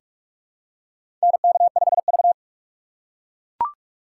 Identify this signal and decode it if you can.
Morse code, heard as a keyed tone, about 44 wpm, 700 Hz. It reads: NK5V
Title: USA Call Signs - 6x Character Spacing - 45wpm